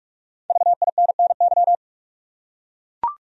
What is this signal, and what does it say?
Morse code, heard as a keyed tone, about 43 wpm, 700 Hz. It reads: VINNY
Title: USA Common Names - 45wpm